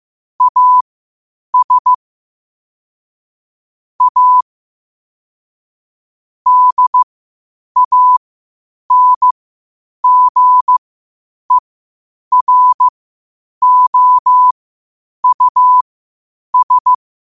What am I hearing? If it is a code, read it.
Morse code, heard as a keyed tone, about 15 wpm, spaced as for 8 wpm, 1000 Hz. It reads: AS A DANGEROUS
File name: qotdfiles / 8wpm